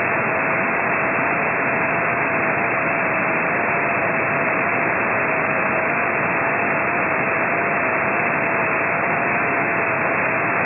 FT8.
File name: ft8-vs-cw / 45-to-150 FT8 -12